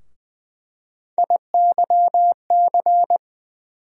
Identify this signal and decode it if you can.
Morse code, heard as a keyed tone, about 20 wpm, 700 Hz. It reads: IYC
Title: USA Call Sign Postfixes - 3 Letters 20wpm